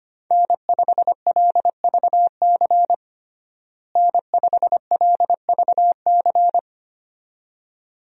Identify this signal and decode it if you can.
Morse code, heard as a keyed tone, about 25 wpm, 700 Hz. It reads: N5LVC N5LVC
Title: USA Call Signs - 5 Characters - Repeated 2x - 25wpm